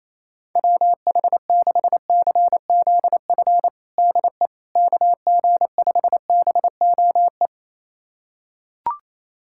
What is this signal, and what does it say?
Morse code, heard as a keyed tone, about 28 wpm, 700 Hz. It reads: WH6CZF DE KG5BOE
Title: USA Callsign DE Callsign - 6 Characters - 28wpm